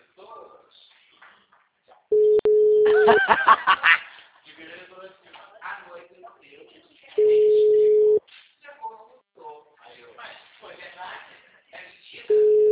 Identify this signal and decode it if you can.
Ringback tone (Europe, 425 Hz), with speech.